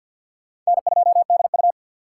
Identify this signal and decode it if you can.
Morse code, heard as a keyed tone, about 64 wpm, 700 Hz. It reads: N2DU